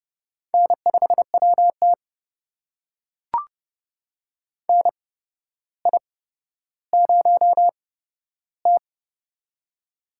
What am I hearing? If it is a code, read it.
Morse code, heard as a keyed tone, about 30 wpm, spaced as for 13 wpm, 700 Hz. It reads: N5WT NI0T
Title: USA Call Signs - 4 Characters - 8x Character Spacing - 30wpm